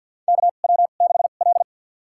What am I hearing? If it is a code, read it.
Morse code, heard as a keyed tone, about 53 wpm, spaced as for 37 wpm, 700 Hz. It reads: KW6L
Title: USA Call Signs - 4 Characters - 2x Character Spacing - 50wpm